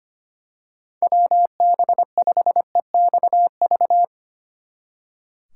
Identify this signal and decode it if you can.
Morse code, heard as a keyed tone, about 27 wpm, 700 Hz. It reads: WB5EXV
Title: Speed Racing - USA 2x3 Call Signs 25wpm